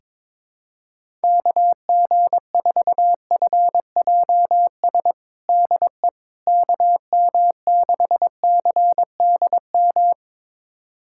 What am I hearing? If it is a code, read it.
Morse code, heard as a keyed tone, about 22 wpm, 700 Hz. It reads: KG4FJS DE KM6CDM
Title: USA Callsign DE Callsign 22wpm